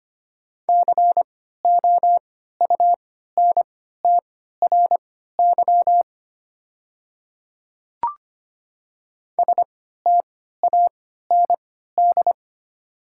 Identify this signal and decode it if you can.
Morse code, heard as a keyed tone, about 25 wpm, spaced as for 14 wpm, 690 Hz. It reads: COUNTRY STAND
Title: 3x Character Spacing - Top 300 Words - 25wpm